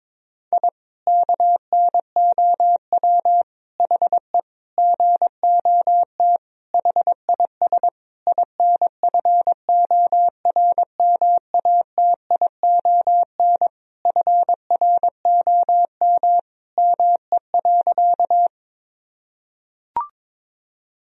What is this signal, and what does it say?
Morse code, heard as a keyed tone, about 22 wpm, 700 Hz. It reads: I KNOW HE GOT HIS INFORMATION FROM ME.
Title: Sentences from Top 1000 Words - Review Part 3 - 22wpm